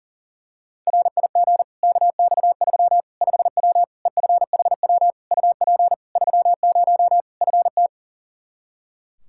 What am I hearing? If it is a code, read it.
Morse code, heard as a keyed tone, about 40 wpm, 700 Hz. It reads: RIG KX3 5W EFHW UP 30 FT
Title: Practice QSO - Exchange 2 - Info - 40wpm